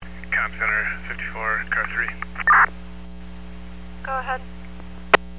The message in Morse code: none

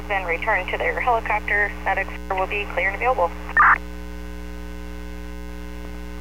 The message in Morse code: none

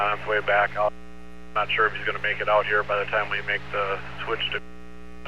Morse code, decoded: none

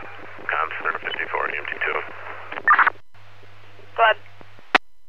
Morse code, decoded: none

